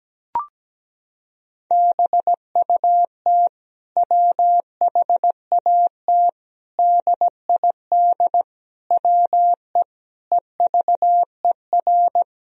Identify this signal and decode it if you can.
Morse code, heard as a keyed tone, about 17 wpm, 700 Hz. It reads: BUT WHAT DID WE EVER
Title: Sentences from Top 400 Words and Abbreviations 17wpm